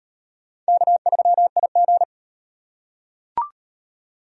Morse code, heard as a keyed tone, about 38 wpm, 700 Hz. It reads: K3IG